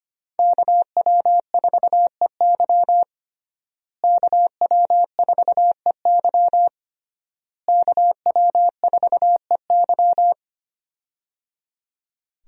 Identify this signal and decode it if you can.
Morse code, heard as a keyed tone, about 25 wpm, 700 Hz. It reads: KW4EY KW4EY KW4EY
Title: USA Call Signs - 5 Characters - Repeated 3x - 25wpm